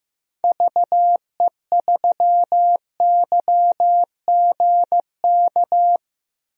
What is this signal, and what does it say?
Morse code, heard as a keyed tone, about 15 wpm, 700 Hz. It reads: VE3YGK
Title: DXCC-100 Call Signs - 6 Characters - Repeated 3x - 15wpm